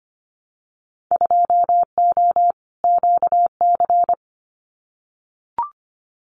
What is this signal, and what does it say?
Morse code, heard as a keyed tone, about 25 wpm, 700 Hz. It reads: 2O QC
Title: Winter Field Day Exchanges 25wpm